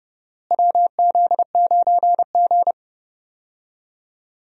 Morse code, heard as a keyed tone, about 30 wpm, 700 Hz. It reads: WZ9G
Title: USA Call Signs - 4 Characters - Repeated 3x - 30wpm